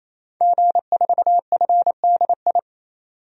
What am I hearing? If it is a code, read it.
Morse code, heard as a keyed tone, about 28 wpm, 700 Hz. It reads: G4FDI
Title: DXCC-100 Call Signs - 5 Characters - 28wpm